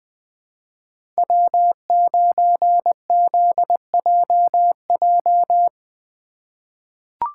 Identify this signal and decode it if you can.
Morse code, heard as a keyed tone, about 20 wpm, 700 Hz. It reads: W9ZJJ
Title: USA Call Signs - 5 Characters - 3x Character Spacing - 20wpm